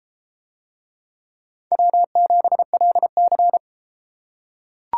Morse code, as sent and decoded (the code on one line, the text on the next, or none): .-- --... .-.. -.-.
W7LC